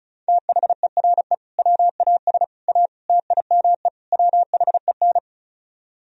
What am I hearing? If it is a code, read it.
Morse code, heard as a keyed tone, about 44 wpm, 700 Hz. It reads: THERE WAS A TIME WHEN